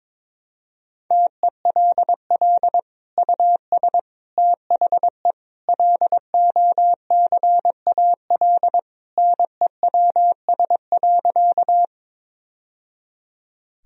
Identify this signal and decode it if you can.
Morse code, heard as a keyed tone, about 22 wpm, 700 Hz. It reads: TELL US THE LOCAL NEWS.
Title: Sentences from Top 1000 Words - Review Part 3 - 22wpm